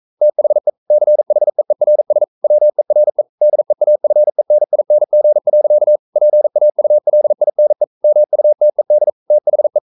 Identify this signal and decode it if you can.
Morse code, heard as a keyed tone, about 42 wpm, 600 Hz. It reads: THE CHEERS WERE DEAFENING. PAULINE MUTED THE
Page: Romanen Never af Ken Follet på engelsk.